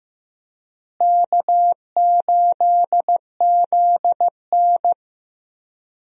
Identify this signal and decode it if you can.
Morse code, heard as a keyed tone, about 15 wpm, 700 Hz. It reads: K8ZN
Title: USA Call Signs - 4 Characters - Repeated 2x - 15wpm